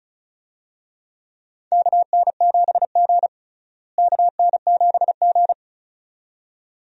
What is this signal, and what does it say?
Morse code, heard as a keyed tone, about 36 wpm, 700 Hz. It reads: KN7G KN7G
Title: USA Call Signs - 4 Characters - Repeated 2x - 35wpm